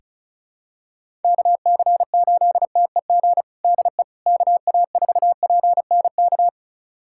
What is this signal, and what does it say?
Morse code, heard as a keyed tone, about 34 wpm, 700 Hz. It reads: KC8TEG DE KA4PNK
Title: USA Callsign DE Callsign 35wpm